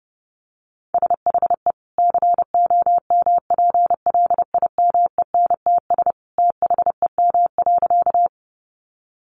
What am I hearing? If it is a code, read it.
Morse code, heard as a keyed tone, about 29 wpm, 700 Hz. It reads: SHE COMPLIMENTS THEM.
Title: Easy Phrases 30wpm